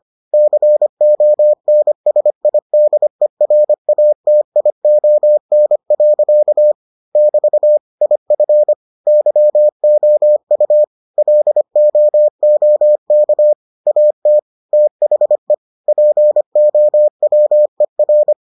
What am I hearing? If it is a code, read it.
Morse code, heard as a keyed tone, about 25 wpm, 600 Hz. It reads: CONSIDERATION. <BT> IF YOU LOOK AT THE POWER